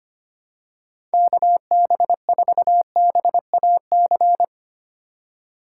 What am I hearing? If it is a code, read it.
Morse code, heard as a keyed tone, about 25 wpm, 700 Hz. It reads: KB4BAC
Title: USA Call Signs - 6 Characters - 4x Character Spacing - 25wpm